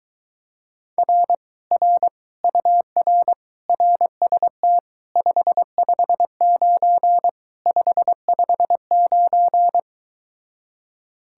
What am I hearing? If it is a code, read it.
Morse code, heard as a keyed tone, about 22 wpm, 700 Hz. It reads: R R UR RST 559 559